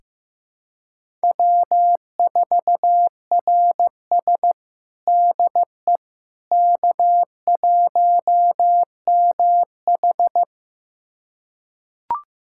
Morse code, heard as a keyed tone, about 15 wpm, 700 Hz. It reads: W4RS DE K1MH
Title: USA Callsign DE Callsign - 4 Characters - 15wpm